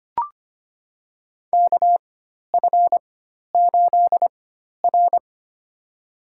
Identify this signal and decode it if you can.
Morse code, heard as a keyed tone, about 25 wpm, spaced as for 12 wpm, 700 Hz. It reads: KF8R
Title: USA Call Signs - 4 Characters - 4x Character Spacing - 25wpm